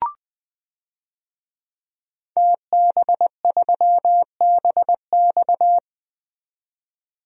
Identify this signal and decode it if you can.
Morse code, heard as a keyed tone, about 20 wpm, 700 Hz. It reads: TB3BX
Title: DXCC-100 Call Signs - 5 Characters - 20wpm